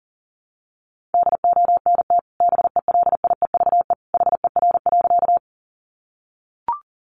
Morse code, heard as a keyed tone, about 40 wpm, 700 Hz. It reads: DONT BELIEVE HER.
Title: Sentences from Top 1000 Words 40wpm